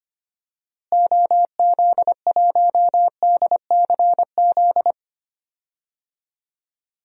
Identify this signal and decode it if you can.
Morse code, heard as a keyed tone, about 25 wpm, 700 Hz. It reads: OZ1DCZ
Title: DXCC-100 Call Signs - 6 Characters - Repeated 2x - 25wpm